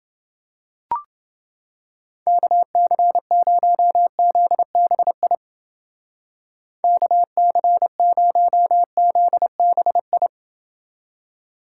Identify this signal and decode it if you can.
Morse code, heard as a keyed tone, about 28 wpm, 700 Hz. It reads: KC0ZBI KC0ZBI